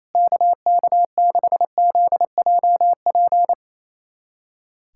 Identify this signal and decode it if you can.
Morse code, heard as a keyed tone, about 28 wpm, 700 Hz. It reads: KK6ZJP
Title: USA Callsign DE Callsign - 6 Characters - 28wpm